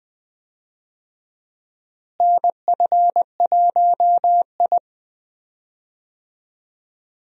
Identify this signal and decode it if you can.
Morse code, heard as a keyed tone, about 20 wpm, 700 Hz. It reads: NF1I
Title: USA Call Signs - 4 Characters - Repeated 2x - 20wpm